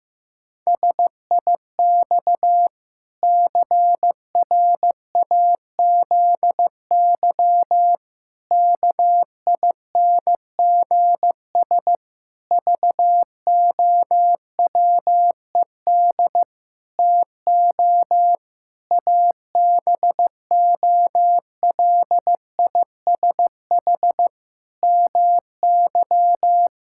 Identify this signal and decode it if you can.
Morse code, heard as a keyed tone, about 15 wpm, 700 Hz. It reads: SIX CRAZY KINGS VOWED TO ABOLISH MY
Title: Pangram Challenge v2 - 15wpm